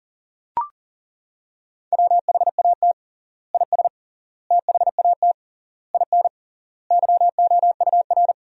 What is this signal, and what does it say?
Morse code, heard as a keyed tone, about 40 wpm, 700 Hz. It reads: WHAT IS THAT IN YOUR
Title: 3x Word-Spacing - Sentences from Top 200 Words - 40wpm